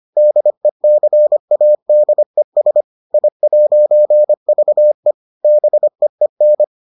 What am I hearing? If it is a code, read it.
Morse code, heard as a keyed tone, about 25 wpm, 600 Hz. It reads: DECADES I'VE BEEN